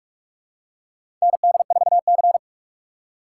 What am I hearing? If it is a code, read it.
Morse code, heard as a keyed tone, about 47 wpm, 700 Hz. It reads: ND4C
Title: USA Call Signs - 4 Characters - 4x Character Spacing - 45wpm